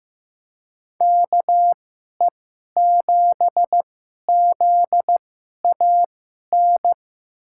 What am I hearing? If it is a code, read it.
Morse code, heard as a keyed tone, about 15 wpm, spaced as for 11 wpm, 700 Hz. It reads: KE7ZAN
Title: USA Call Signs - 2x Character Spacing - 15wpm